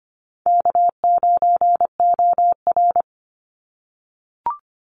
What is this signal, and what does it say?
Morse code, heard as a keyed tone, about 25 wpm, 700 Hz. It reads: K9OR
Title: Easy Call Signs - 25wpm